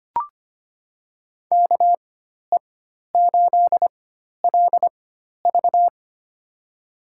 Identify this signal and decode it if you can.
Morse code, heard as a keyed tone, about 25 wpm, spaced as for 12 wpm, 700 Hz. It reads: KE8LV
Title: USA Call Signs - 5 Characters - 4x Character Spacing - 25wpm